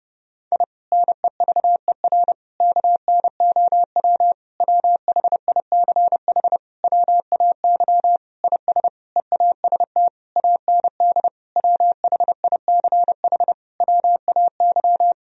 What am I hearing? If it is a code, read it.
Morse code, heard as a keyed tone, about 30 wpm, 700 Hz. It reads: I NEVER KNOW WHICH WAY IS EAST AND WHICH WAY
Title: Sentences from Top 1000 Words - Review Part 3 - 30wpm